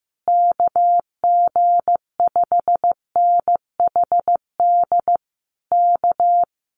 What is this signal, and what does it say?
Morse code, heard as a keyed tone, about 15 wpm, 700 Hz. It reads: KG5NHD K
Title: Common QSO Phrases Encore 15wpm